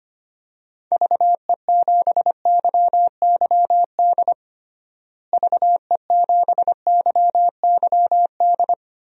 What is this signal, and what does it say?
Morse code, heard as a keyed tone, about 25 wpm, 700 Hz. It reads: VE7YYD VE7YYD
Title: DXCC-100 Call Signs - 6 Characters - Repeated 2x - 25wpm